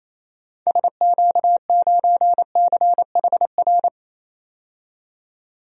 Morse code, heard as a keyed tone, about 27 wpm, 700 Hz. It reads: SQ9CHR
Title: DXCC-100 Call Signs - 6 Characters - 28wpm